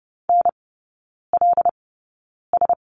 Morse code, heard as a keyed tone, about 30 wpm, spaced as for 9 wpm, 700 Hz. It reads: NLS
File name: course / 30:10wpm / 087 Lesson 15e - Number 5 - Three Characters 30wpm